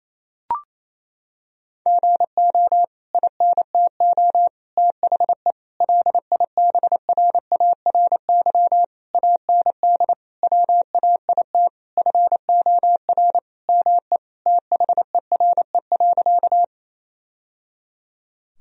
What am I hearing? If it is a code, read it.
Morse code, heard as a keyed tone, about 28 wpm, 700 Hz. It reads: GO INTO THE LIBRARY AND WAIT FOR ME THERE.
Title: Sentences from Top 1000 Words - Review Part 4 - 28wpm